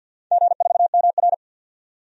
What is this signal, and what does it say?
Morse code, heard as a keyed tone, about 53 wpm, 700 Hz. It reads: G4MR
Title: DXCC-100 Call Signs - 4 Characters - 50wpm